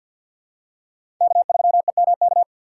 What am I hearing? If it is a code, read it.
Morse code, heard as a keyed tone, about 48 wpm, 700 Hz. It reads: K3EMK
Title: USA Call Signs - 5 Characters - 3x Character Spacing - 50wpm